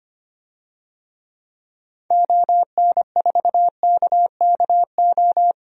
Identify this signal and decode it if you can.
Morse code, heard as a keyed tone, about 25 wpm, 700 Hz. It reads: ON4KKO
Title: DXCC-100 Call Signs - 6 Characters - 25wpm